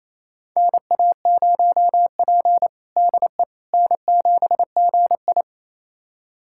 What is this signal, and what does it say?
Morse code, heard as a keyed tone, about 28 wpm, 700 Hz. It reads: NA0P DE N7GI
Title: USA Callsign DE Callsign - 4 Characters - 28wpm